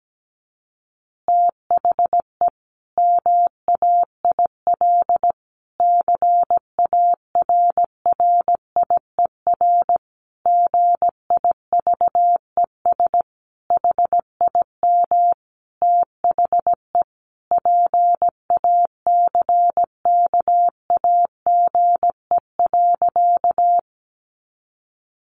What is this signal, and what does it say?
Morse code, heard as a keyed tone, about 17 wpm, 700 Hz. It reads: THE MAIL CARRIER GIVES HIM THE PACKAGE.
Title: Additional Easy Phrases 17wpm